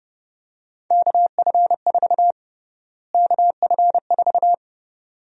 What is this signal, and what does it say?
Morse code, heard as a keyed tone, about 30 wpm, 700 Hz. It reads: KF4 KF4
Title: USA Call Sign Prefixes - Repeated 2x - 30wpm